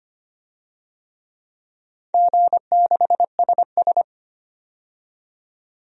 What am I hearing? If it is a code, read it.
Morse code, heard as a keyed tone, about 25 wpm, 700 Hz. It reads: G6SS
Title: DXCC-100 Call Signs - 4 Characters - 25wpm